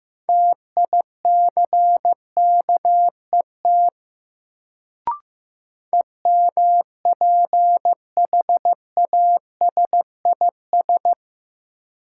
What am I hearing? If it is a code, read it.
Morse code, heard as a keyed tone, about 15 wpm, 700 Hz. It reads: TICKET EMPHASIS
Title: Sentences from Top 1400 Words - New Word Drill Alt - 15wpm